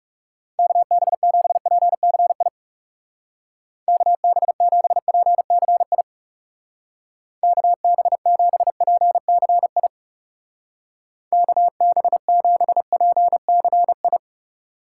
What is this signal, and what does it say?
Morse code, heard as a keyed tone, about 36 wpm, 700 Hz. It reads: KB7PCI KB7PCI KB7PCI KB7PCI
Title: Speed Racing - USA Call Signs 30wpm